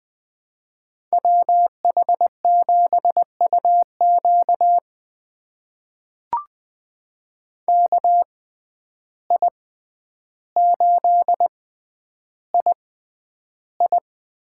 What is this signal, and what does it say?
Morse code, heard as a keyed tone, about 20 wpm, spaced as for 10 wpm, 700 Hz. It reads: WH7UQ KI8II
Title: USA Call Signs - 5 Characters - 6x Character Spacing - 20wpm